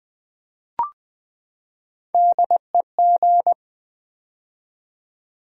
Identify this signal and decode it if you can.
Morse code, heard as a keyed tone, about 20 wpm, 700 Hz. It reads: DEG
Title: Build A Word - 3 Letters At A Time - 6-Letter Words - 20wpm